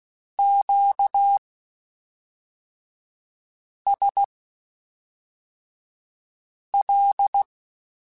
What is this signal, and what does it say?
Morse code, heard as a keyed tone, about 16 wpm, spaced as for 3 wpm, 800 Hz. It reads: QSL
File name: (spaced) / Q-Calls Spaced-mp3s / Q Calls Twice sp @5